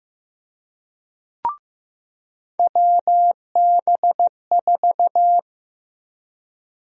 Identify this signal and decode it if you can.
Morse code, heard as a keyed tone, about 15 wpm, 700 Hz. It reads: WB4
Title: USA Call Sign Prefixes - 15wpm